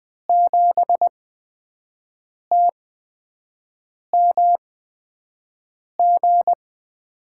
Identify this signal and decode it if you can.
Morse code, heard as a keyed tone, about 20 wpm, spaced as for 5 wpm, 700 Hz. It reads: 7TMG
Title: USA Call Signs - 8x Character Spacing - 20wpm